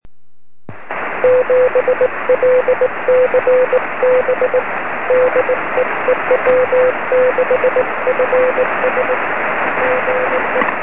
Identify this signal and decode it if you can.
Morse code, heard as a keyed tone, about 19 wpm, 510 Hz, with noise. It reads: 7LCB DE EW6FS GE
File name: sm7lcb / 20 23cm / 1659 ew6fs 23cm